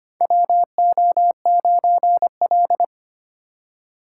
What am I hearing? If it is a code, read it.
Morse code, heard as a keyed tone, about 25 wpm, 700 Hz. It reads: WO9L
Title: USA Call Signs - 4 Characters - 4x Character Spacing - 25wpm